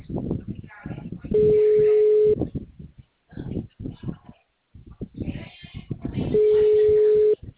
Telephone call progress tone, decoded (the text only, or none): ringback tone